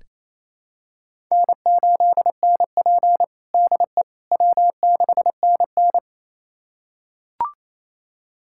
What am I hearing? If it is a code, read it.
Morse code, heard as a keyed tone, about 28 wpm, 700 Hz. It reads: N8NP DE W6NN
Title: USA Callsign DE Callsign - 4 Characters - 28wpm